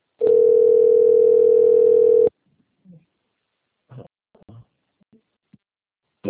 Ringback tone (North America), 440 + 480 Hz.
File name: rg-1600-unknown- - - .28725